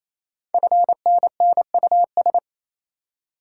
Morse code, heard as a keyed tone, about 28 wpm, 700 Hz. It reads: FNNUS
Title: Code Groups - 5 Letters - 28wpm